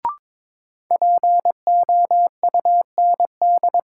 Morse code, heard as a keyed tone, about 22 wpm, 700 Hz. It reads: POUND
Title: Sentences from Top 1000 Words - Review Part 2 - New Word Drill - 22wpm